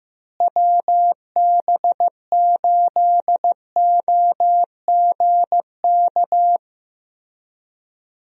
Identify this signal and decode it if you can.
Morse code, heard as a keyed tone, about 15 wpm, 700 Hz. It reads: WB8OGK
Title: USA Call Signs - 6 Characters - Repeated 3x - 15wpm